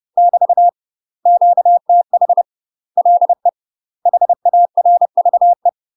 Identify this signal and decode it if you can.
Morse code, heard as a keyed tone, about 30 wpm, 700 Hz. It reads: <BT> QTH LE HARVE
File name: UK MASTER QSOs TRAINING / 30WPM / UK MASTER QSOs 0009